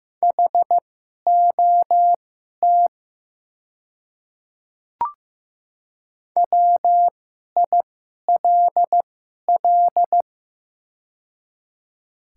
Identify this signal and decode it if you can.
Morse code, heard as a keyed tone, about 15 wpm, spaced as for 11 wpm, 700 Hz. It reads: HOT WILL
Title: 2x Character Spacing - Top 100 Words - 15wpm